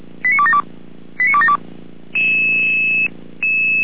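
Five-tone selective call (ZVEI), heard as quick runs of five tones, tones of about 70 ms. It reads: 89272 89272
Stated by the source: ZVEI five-tone selective call, received on an SDR.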